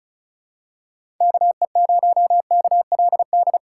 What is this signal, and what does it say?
Morse code, heard as a keyed tone, about 34 wpm, 700 Hz. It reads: KE0KLD
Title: USA Call Signs - 6 Characters - 2x Character Spacing - 35wpm